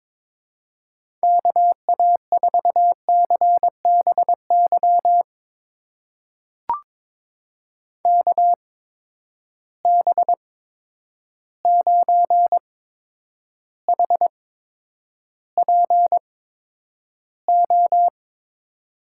Morse code, heard as a keyed tone, about 22 wpm, spaced as for 9 wpm, 700 Hz. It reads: KA4CBY KB9HPO
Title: USA Call Signs - 6 Characters - 8x Character Spacing - 22wpm